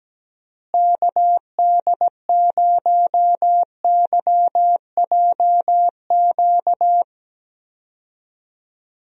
Morse code, heard as a keyed tone, about 17 wpm, 700 Hz. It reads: KD0YJQ
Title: USA Call Signs - 6 Characters - Repeated 3x - 17wpm